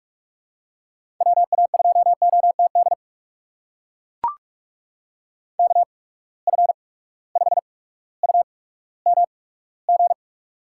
Morse code, heard as a keyed tone, about 44 wpm, spaced as for 19 wpm, 700 Hz. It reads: WA2OTD KF5UMG
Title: USA Call Signs - 6 Characters - 8x Character Spacing - 45wpm